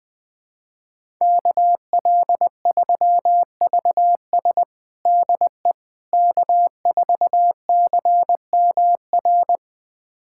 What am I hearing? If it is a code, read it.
Morse code, heard as a keyed tone, about 20 wpm, 700 Hz. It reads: KL3VS DE K4CMR
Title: USA Callsign DE Callsign - 5 Characters - 20wpm